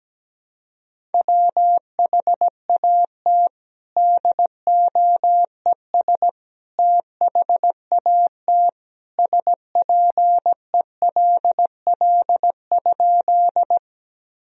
Morse code, heard as a keyed tone, about 17 wpm, 700 Hz. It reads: WHAT DOES THAT SPELL?
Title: Sentences from Top 1000 Words - Review Part 1 - 17wpm